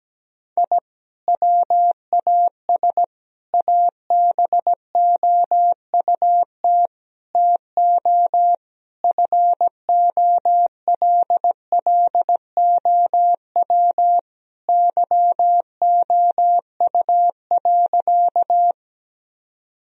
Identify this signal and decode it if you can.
Morse code, heard as a keyed tone, about 17 wpm, 700 Hz. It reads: I WAS ABOUT TO FOLLOW YOU.
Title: Sentences from Top 1000 Words - Review Part 1 - 17wpm